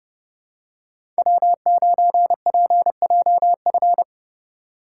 Morse code, heard as a keyed tone, about 30 wpm, 700 Hz. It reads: W9PJF
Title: USA Call Signs - 5 Characters - 2x Character Spacing - 30wpm